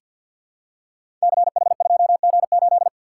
Morse code, heard as a keyed tone, about 47 wpm, 700 Hz. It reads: CH2G8
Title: Code Groups - 5 Letter:Numbers - 50wpm